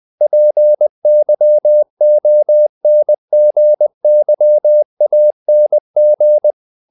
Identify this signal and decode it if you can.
Morse code, heard as a keyed tone, about 20 wpm, 600 Hz. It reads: PYONGYANG
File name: CW / Practice / NationalCapitals20